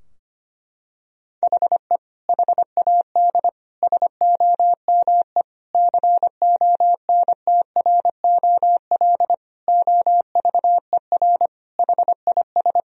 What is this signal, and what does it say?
Morse code, heard as a keyed tone, about 25 wpm, 700 Hz. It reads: HE HAD SOME CONTROL OVER HIS
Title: Sentences from Top 1000 Words - Review Part 3 - 25wpm